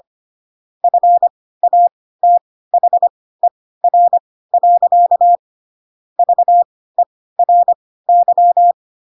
Morse code, heard as a keyed tone, about 25 wpm, spaced as for 16 wpm, 700 Hz. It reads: FATHER. VERY
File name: Pinocchio / Pinocchio ch3 10 25 ik1hge